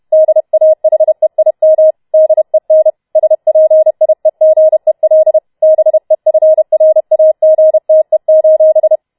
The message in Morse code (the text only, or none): DAHEIM DEN SPIEGEL BEFRAGTE: